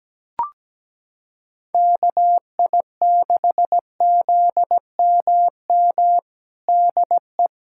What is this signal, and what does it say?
Morse code, heard as a keyed tone, about 17 wpm, 700 Hz. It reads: KI6ZMM DE
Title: USA Callsign DE Callsign 17wpm